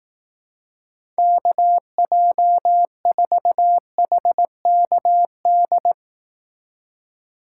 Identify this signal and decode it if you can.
Morse code, heard as a keyed tone, about 19 wpm, 700 Hz. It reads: KJ4HKD
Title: Speed Racing - USA 2x3 Call Signs 15wpm